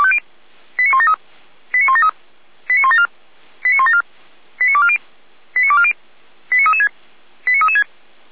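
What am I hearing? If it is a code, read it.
Five-tone selective call (ZVEI), heard as quick runs of five tones, tones of about 70 ms. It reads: 89173 89173 89174 89174 89240 89240 89336 89336
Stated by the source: ZVEI five-tone call received on SDR.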